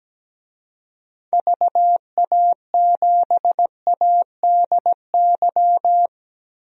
Morse code, heard as a keyed tone, about 17 wpm, 700 Hz. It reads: VA7ADY
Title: DXCC-100 Call Signs - 6 Characters - 17wpm